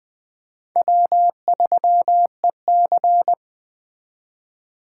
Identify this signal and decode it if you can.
Morse code, heard as a keyed tone, about 20 wpm, 700 Hz. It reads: W3EC
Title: Speed Racing - USA 1x2 Call Signs 17wpm